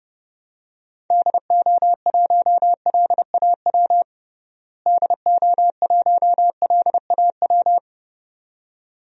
Morse code, heard as a keyed tone, about 30 wpm, 700 Hz. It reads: DO1LAW DO1LAW
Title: DXCC-100 Call Signs - 6 Characters - Repeated 2x - 30wpm